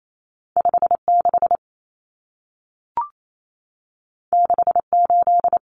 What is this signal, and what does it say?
Morse code, heard as a keyed tone, about 28 wpm, 700 Hz. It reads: 56 68
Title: Numbers - 2-Digit 28wpm